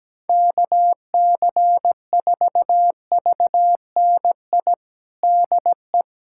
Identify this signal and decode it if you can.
Morse code, heard as a keyed tone, about 17 wpm, 700 Hz. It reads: KC4VNI DE
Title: USA Callsign DE Callsign 17wpm